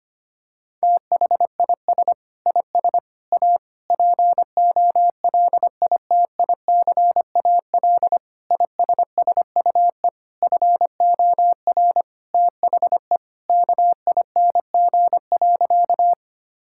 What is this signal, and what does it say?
Morse code, heard as a keyed tone, about 25 wpm, 700 Hz. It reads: THIS IS A POLITICAL ISSUE FOR THE KING.
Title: Sentences from Top 1000 Words - Review Part 6 - 25wpm